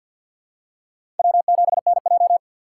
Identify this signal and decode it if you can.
Morse code, heard as a keyed tone, about 53 wpm, 700 Hz. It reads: W7NJ
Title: USA Call Signs - 6x Character Spacing - 50wpm